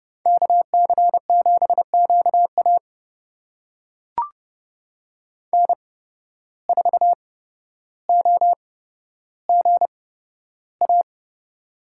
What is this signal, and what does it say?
Morse code, heard as a keyed tone, about 30 wpm, spaced as for 13 wpm, 700 Hz. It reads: KC7QA N4OGA
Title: USA Call Signs - 5 Characters - 8x Character Spacing - 30wpm